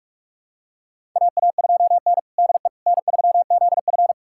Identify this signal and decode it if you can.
Morse code, heard as a keyed tone, about 46 wpm, 700 Hz. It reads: AA2N DE N3ZF
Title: USA Callsign DE Callsign - 4 Characters - 45wpm